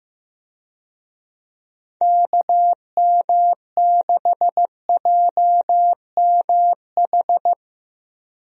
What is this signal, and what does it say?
Morse code, heard as a keyed tone, about 15 wpm, 700 Hz. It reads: KM6JMH
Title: USA Call Signs - 6 Characters - Repeated 3x - 15wpm